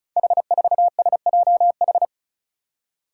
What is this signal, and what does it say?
Morse code, heard as a keyed tone, about 36 wpm, 700 Hz. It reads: H4SJH
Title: Five Challenging Letter-Number Combinations 35wpm